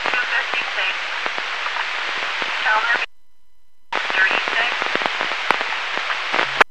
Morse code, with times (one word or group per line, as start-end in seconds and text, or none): none